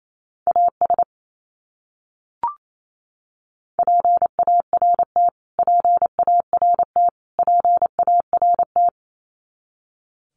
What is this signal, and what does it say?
Morse code, heard as a keyed tone, about 28 wpm, 700 Hz. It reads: AS PART PART PART
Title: Top 200 Words Repeated 28wpm